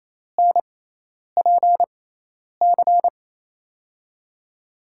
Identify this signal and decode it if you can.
Morse code, heard as a keyed tone, about 28 wpm, spaced as for 10 wpm, 700 Hz. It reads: NPC
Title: USA Call Signs - 6 Characters - 6x Character Spacing - 28wpm